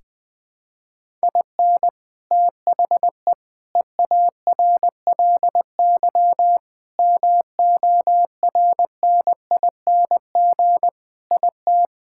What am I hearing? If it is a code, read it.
Morse code, heard as a keyed tone, about 20 wpm, 700 Hz. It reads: IN THE EARLY MORNING IT
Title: Sentences with 3-Word Phrases - 20wpm